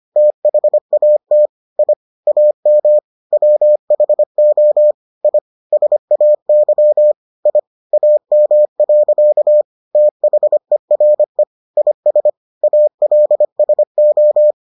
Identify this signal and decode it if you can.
Morse code, heard as a keyed tone, about 25 wpm, 600 Hz. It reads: THAT I AM WHO I SAY I AM. THERE IS ALSO